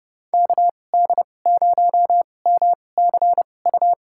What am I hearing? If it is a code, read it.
Morse code, heard as a keyed tone, about 30 wpm, spaced as for 22 wpm, 700 Hz. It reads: KD0MCU
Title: USA Call Signs - 2x Character Spacing - 30wpm